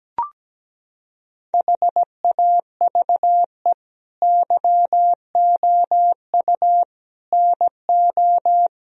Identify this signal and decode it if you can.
Morse code, heard as a keyed tone, about 17 wpm, 700 Hz. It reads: HAVE YOU NO
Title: Sentences from Top 1000 Words - Review Part 1 - 17wpm